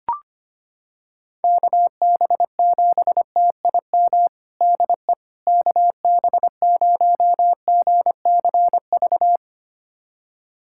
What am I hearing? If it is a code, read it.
Morse code, heard as a keyed tone, about 25 wpm, 700 Hz. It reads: KB7TIM DE KB0GCV
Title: USA Callsign DE Callsign - 6 Characters - 25wpm